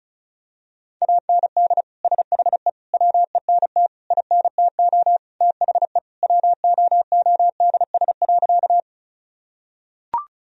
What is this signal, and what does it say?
Morse code, heard as a keyed tone, about 36 wpm, 700 Hz. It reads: AND SHE WENT INTO THE WOODS.
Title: Sentences from Top 1000 Words - Review Part 2 - 35wpm